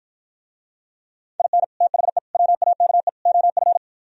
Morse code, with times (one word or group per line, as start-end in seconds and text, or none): 1.4-1.6 IN
1.8-2.2 THE
2.3-3.1 WAKE
3.3-3.8 OF